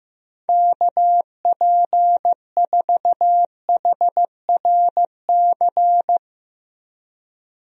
Morse code, heard as a keyed tone, about 15 wpm, 700 Hz. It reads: KP4HRC
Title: USA Call Signs - 6 Characters - Repeated 2x - 15wpm